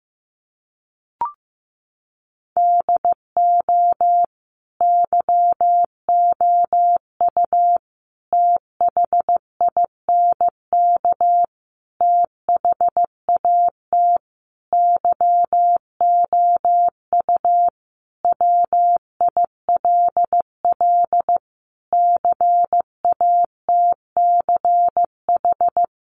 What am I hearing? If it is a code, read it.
Morse code, heard as a keyed tone, about 15 wpm, 700 Hz. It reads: DO YOU THINK THAT YOU WILL CATCH
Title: Sentences from Top 1500 Words 15wpm